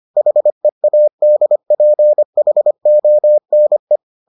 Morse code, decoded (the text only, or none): HEADPHONE